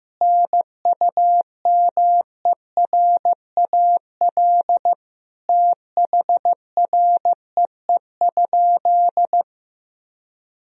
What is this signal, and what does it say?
Morse code, heard as a keyed tone, about 15 wpm, 700 Hz. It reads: NUMERAL THREE?
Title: Sentences from Top 1000 Words - Review Part 2 - 15wpm